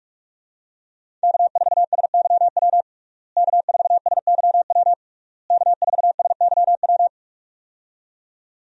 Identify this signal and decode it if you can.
Morse code, heard as a keyed tone, about 46 wpm, 700 Hz. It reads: K4SYW K4SYW K4SYW
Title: USA Call Signs - 5 Characters - Repeated 3x - 45wpm